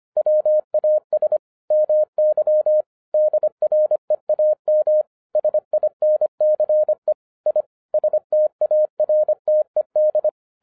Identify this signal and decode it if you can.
Morse code, heard as a keyed tone, about 25 wpm, 610 Hz. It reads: WAS MY DREAM SINCE I STARTED
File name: 2011-10-mp3 / n04-JUST MOVED 25wpm